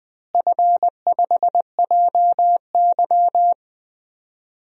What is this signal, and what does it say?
Morse code, heard as a keyed tone, about 21 wpm, 700 Hz. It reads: F5JY
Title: DXCC-100 Call Signs - 4 Characters - Repeated 2x - 20wpm